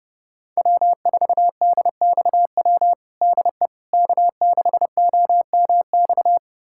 Morse code, heard as a keyed tone, about 30 wpm, 700 Hz. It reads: W4DXW DE K6OMX
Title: USA Callsign DE Callsign - 5 Characters - 30wpm